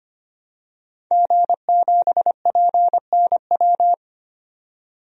Morse code, heard as a keyed tone, about 25 wpm, 700 Hz. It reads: G7PNW
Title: DXCC-100 Call Signs - 5 Characters - 25wpm